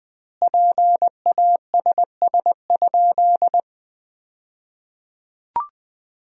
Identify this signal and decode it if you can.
Morse code, heard as a keyed tone, about 20 wpm, 700 Hz. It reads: PASS?
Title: 2x Word-Spacing - Sentences from Top 400 Words - 20wpm